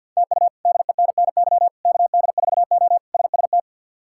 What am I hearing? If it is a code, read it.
Morse code, heard as a keyed tone, about 47 wpm, 700 Hz. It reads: TU DENNY KD4O SST
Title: K1USN SST QSO Practice - 50wpm